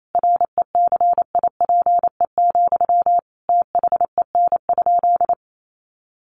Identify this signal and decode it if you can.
Morse code, heard as a keyed tone, about 28 wpm, 700 Hz. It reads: RECIPE, THEN?
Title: Sentences from Top 1200 Words 28wpm